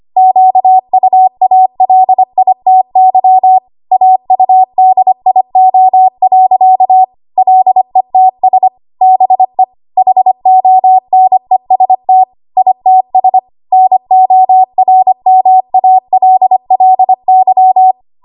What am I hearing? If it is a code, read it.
Morse code, heard as a keyed tone, about 25 wpm, 750 Hz. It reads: QUALITY AUDIO. LETS BE HONEST ITS NORMALLY
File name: Archive / 25 WPM / 25WPM